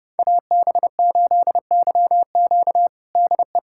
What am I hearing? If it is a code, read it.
Morse code, heard as a keyed tone, about 30 wpm, 700 Hz. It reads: AB8YQ DE
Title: USA Callsign DE Callsign 30wpm